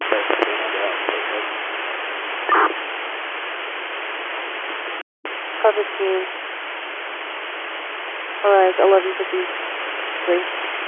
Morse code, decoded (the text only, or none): none